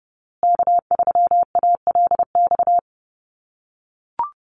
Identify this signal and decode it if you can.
Morse code, heard as a keyed tone, about 30 wpm, 700 Hz. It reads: K3ALX
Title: USA Call Signs Repeated 3x 30wpm